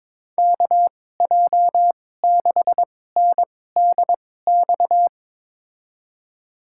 Morse code, heard as a keyed tone, about 21 wpm, spaced as for 16 wpm, 700 Hz. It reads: KJ6NDX
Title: USA Call Signs - 6 Characters - 2x Character Spacing - 22wpm